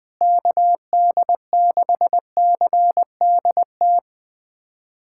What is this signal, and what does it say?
Morse code, heard as a keyed tone, about 20 wpm, 700 Hz. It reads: KD6CDT
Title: USA Callsign DE Callsign 20wpm